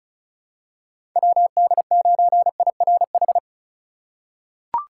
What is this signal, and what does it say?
Morse code, heard as a keyed tone, about 36 wpm, 700 Hz. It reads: WD9IRH
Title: USA Call Signs - 6 Characters - 3x Character Spacing - 35wpm